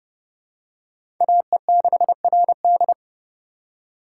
Morse code, heard as a keyed tone, about 32 wpm, 700 Hz. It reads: AE6RD
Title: USA Call Signs - 5 Characters - 3x Character Spacing - 30wpm